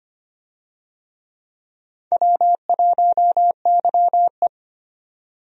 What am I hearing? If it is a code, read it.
Morse code, heard as a keyed tone, about 25 wpm, 700 Hz. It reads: W1YE
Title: USA Call Signs - 4 Characters - 25wpm